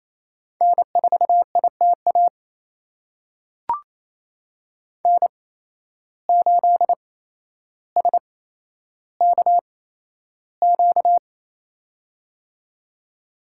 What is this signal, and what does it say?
Morse code, heard as a keyed tone, about 28 wpm, spaced as for 12 wpm, 700 Hz. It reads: N4ITA N8SKQ
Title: USA Call Signs - 5 Characters - 8x Character Spacing - 28wpm